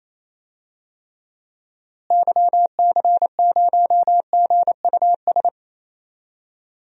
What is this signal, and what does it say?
Morse code, heard as a keyed tone, about 28 wpm, 700 Hz. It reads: YC0GUS
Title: DXCC-100 Call Signs - 6 Characters - Repeated 3x - 28wpm